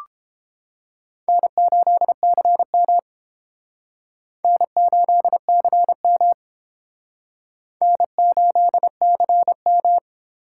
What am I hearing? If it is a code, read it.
Morse code, heard as a keyed tone, about 28 wpm, 700 Hz. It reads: N8CM N8CM N8CM